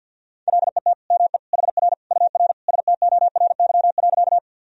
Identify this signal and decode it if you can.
Morse code, heard as a keyed tone, about 52 wpm, 700 Hz. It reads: LET ME HR UR STORY.
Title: Sentences from Top 300 Words and Abbreviations 50wpm